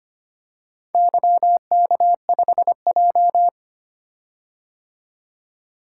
Morse code, heard as a keyed tone, about 25 wpm, 700 Hz. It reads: YK5J
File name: practice / 25wpm / Four Letter-Number Combinations - 25wpm